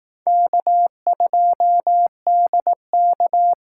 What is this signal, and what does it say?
Morse code, heard as a keyed tone, about 18 wpm, 700 Hz. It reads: K2DK
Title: Speed Racing - USA 1x2 Call Signs 15wpm